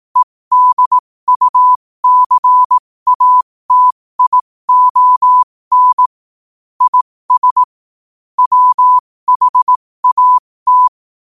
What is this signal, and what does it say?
Morse code, heard as a keyed tone, about 18 wpm, spaced as for 15 wpm, 1000 Hz. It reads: EDUCATION IS WHAT
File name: 15wpm